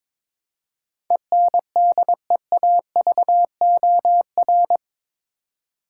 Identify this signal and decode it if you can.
Morse code, heard as a keyed tone, about 22 wpm, 700 Hz. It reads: ENDEAVOR
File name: practice / 22wpm / Eight Letter Words - 22wpm